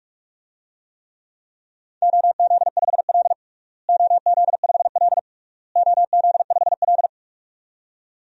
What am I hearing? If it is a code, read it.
Morse code, heard as a keyed tone, about 46 wpm, 700 Hz. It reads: OZ5L OZ5L OZ5L
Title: DXCC-100 Call Signs - 4 Characters - Repeated 3x - 45wpm